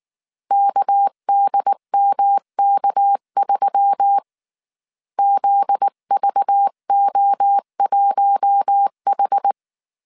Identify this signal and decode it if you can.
Morse code, heard as a keyed tone, about 19 wpm, 800 Hz. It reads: KDMK3 ZVO1H
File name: CW / DET / 1 9 1500C S C 18wpm 2539